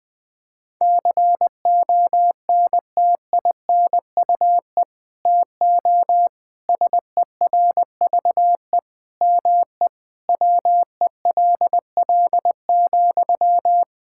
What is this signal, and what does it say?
Morse code, heard as a keyed tone, about 20 wpm, 700 Hz. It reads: CONTINUE TO SERVE ME WELL,
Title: Sentences from Top 1000 Words - Review Part 6 - 20wpm